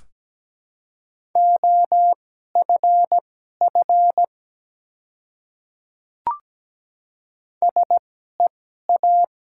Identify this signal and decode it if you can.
Morse code, heard as a keyed tone, about 17 wpm, spaced as for 12 wpm, 700 Hz. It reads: OFF SEA
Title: 2x Character Spacing - Top 300 Words - 17wpm